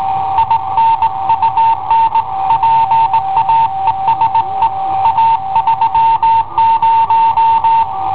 Morse code, heard as a keyed tone, about 18 wpm, 910 Hz. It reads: IN UN PAESE A 30
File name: i7ohp / cassetta i7ohp B